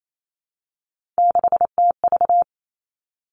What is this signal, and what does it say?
Morse code, heard as a keyed tone, about 28 wpm, 690 Hz. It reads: 6TV
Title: Three Challenging Letter-Number Combinations 28wpm